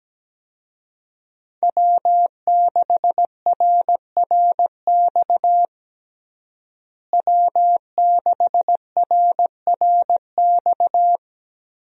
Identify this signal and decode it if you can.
Morse code, heard as a keyed tone, about 17 wpm, 700 Hz. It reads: W6RRX W6RRX
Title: USA Call Signs - 5 Characters - Repeated 2x - 17wpm